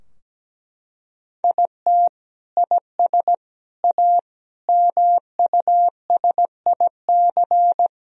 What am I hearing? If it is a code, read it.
Morse code, heard as a keyed tone, about 17 wpm, 700 Hz. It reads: IT IS A MUSIC
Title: Sentences from Top 1000 Words - Review Part 4 - 17wpm